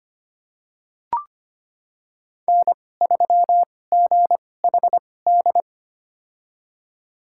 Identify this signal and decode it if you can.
Morse code, heard as a keyed tone, about 25 wpm, spaced as for 18 wpm, 700 Hz. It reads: N3GHD
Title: USA Call Signs - 5 Characters - 2x Character Spacing - 25wpm